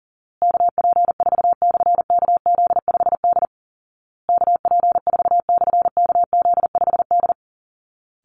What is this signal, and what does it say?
Morse code, heard as a keyed tone, about 40 wpm, 700 Hz. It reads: KP4/KZ5D KP4/KZ5D
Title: Call Signs Repeated 2x - 40wpm